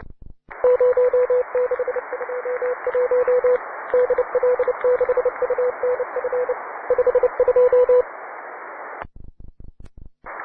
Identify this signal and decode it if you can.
Morse code, heard as a keyed tone, about 29 wpm, 500 Hz, with radio noise. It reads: 0621 DL6UNF 52